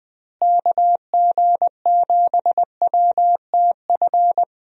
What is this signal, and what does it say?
Morse code, heard as a keyed tone, about 20 wpm, 700 Hz. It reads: KG7WTF
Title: USA Call Signs - 6 Characters - Repeated 2x - 20wpm